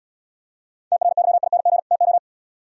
Morse code, heard as a keyed tone, about 71 wpm, 700 Hz. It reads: TRYING TO